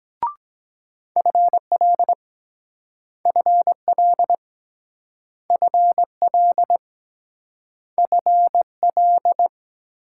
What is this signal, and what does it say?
Morse code, heard as a keyed tone, about 22 wpm, 700 Hz. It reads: FL FL FL FL